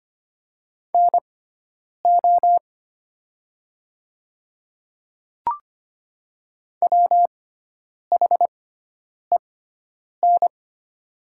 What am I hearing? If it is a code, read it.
Morse code, heard as a keyed tone, about 25 wpm, spaced as for 9 wpm, 700 Hz. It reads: NO WHEN
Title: 6x Character Spacing - Top 100 Words - 25wpm